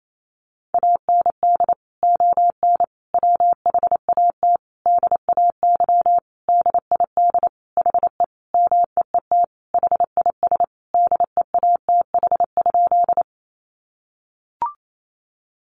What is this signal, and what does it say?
Morse code, heard as a keyed tone, about 28 wpm, 700 Hz. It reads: AND ON WHAT DAY DID HE MEET HIS DEATH?
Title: Sentences from Top 1100 Words 28wpm